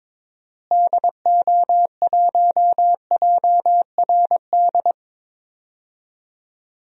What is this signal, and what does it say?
Morse code, heard as a keyed tone, about 22 wpm, 700 Hz. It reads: DO1JRD
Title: DXCC-100 Call Signs - 6 Characters - Repeated 3x - 22wpm